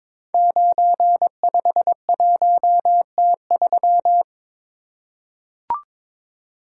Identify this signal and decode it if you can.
Morse code, heard as a keyed tone, about 23 wpm, 700 Hz. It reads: 951T3
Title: Code Groups - 5 Letter:Numbers - 22wpm